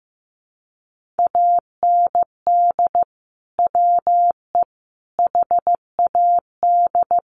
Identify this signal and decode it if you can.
Morse code, heard as a keyed tone, about 15 wpm, 700 Hz. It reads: AND WE HAD
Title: Sentences from Top 1500 Words 15wpm